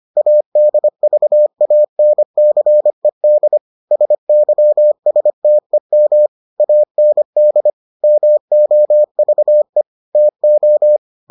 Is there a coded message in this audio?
Yes: Morse code from 0.2 to 11.0 s, heard as a keyed tone, about 25 wpm, 600 Hz. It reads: ADVANCED SYSTEM AND MOVE TO